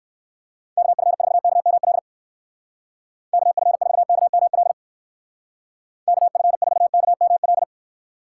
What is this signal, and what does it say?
Morse code, heard as a keyed tone, about 60 wpm, 700 Hz. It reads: KF4KML KF4KML KF4KML